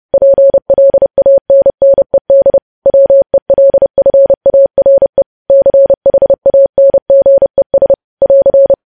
Morse code, heard as a keyed tone, about 30 wpm, 550 Hz. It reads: PLANNED WELFARE CHANGES <AR>
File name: podcast / News headlines-1x-30x30